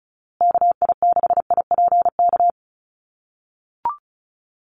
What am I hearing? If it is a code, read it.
Morse code, heard as a keyed tone, about 34 wpm, 700 Hz. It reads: KI6IPK
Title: USA Call Signs Repeated 2x 35wpm